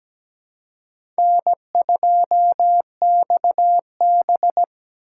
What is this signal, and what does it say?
Morse code, heard as a keyed tone, about 17 wpm, 700 Hz. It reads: N2XB
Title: USA Call Signs - 4 Characters - 4x Character Spacing - 17wpm